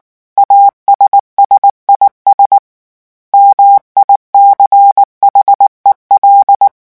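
Morse code, heard as a keyed tone, about 19 wpm, 800 Hz. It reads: ASSIS MICHEL
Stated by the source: French Morse revision 8 drill of random five-letter groups at 18 words per minute.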